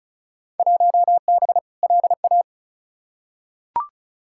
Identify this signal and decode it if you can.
Morse code, heard as a keyed tone, about 36 wpm, 700 Hz. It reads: 1B LA
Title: Field Day Exchanges - 35wpm